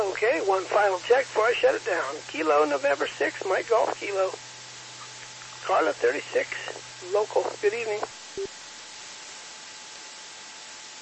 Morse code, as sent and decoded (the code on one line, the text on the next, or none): none